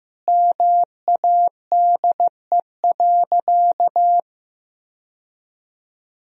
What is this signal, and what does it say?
Morse code, heard as a keyed tone, about 15 wpm, 700 Hz. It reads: MADE.
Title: Sentences from Top 1000 Words - Review Part 4 - 15wpm